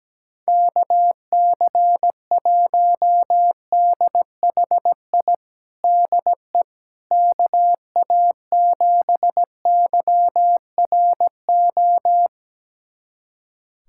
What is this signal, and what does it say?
Morse code, heard as a keyed tone, about 17 wpm, 700 Hz. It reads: KC1DHI DE KA7YRO
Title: USA Callsign DE Callsign - 6 Characters - 17wpm